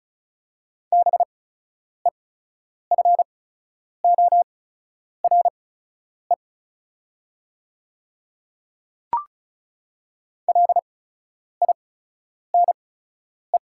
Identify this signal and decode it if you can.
Morse code, heard as a keyed tone, about 36 wpm, spaced as for 10 wpm, 690 Hz. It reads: BEFORE LINE
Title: 8x Character Spacing - Top 300 Words - 35wpm